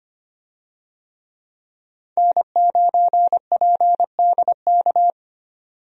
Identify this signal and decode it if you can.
Morse code, heard as a keyed tone, about 25 wpm, 700 Hz. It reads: N9PDK
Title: USA Call Signs - 5 Characters - 25wpm